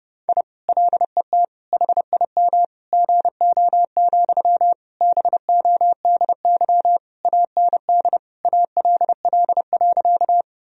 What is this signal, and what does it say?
Morse code, heard as a keyed tone, about 30 wpm, 700 Hz. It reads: I LET HIM GO, BODY AND ALL.
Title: Sentences from Top 1000 Words - Review Part 2 - 30wpm